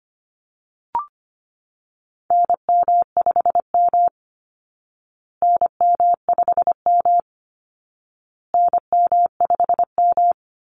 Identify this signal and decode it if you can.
Morse code, heard as a keyed tone, about 25 wpm, 700 Hz. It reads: NM5M NM5M NM5M
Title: Call Signs Repeated 2x - 25wpm